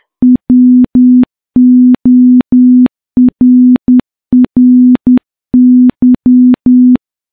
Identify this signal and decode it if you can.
Morse code, heard as a keyed tone, about 10 wpm, 250 Hz. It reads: WORRY
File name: morse / morse-training-10WPM